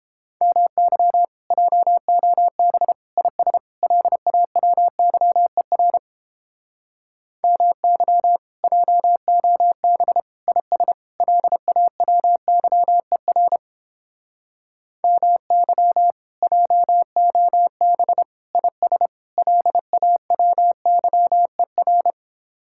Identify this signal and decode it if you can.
Morse code, heard as a keyed tone, about 31 wpm, 700 Hz. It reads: MY JOB IS LAWYER MY JOB IS LAWYER MY JOB IS LAWYER